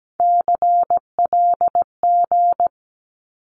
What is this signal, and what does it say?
Morse code, heard as a keyed tone, about 17 wpm, 700 Hz. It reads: CLG
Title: Abbreviations 17wpm